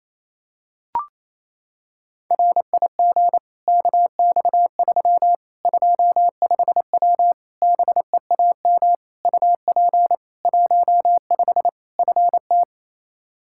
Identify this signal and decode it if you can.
Morse code, heard as a keyed tone, about 28 wpm, 700 Hz. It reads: RIG KX3 25W BEAM UP 15 FT
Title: K1USN SST QSO Practice 28wpm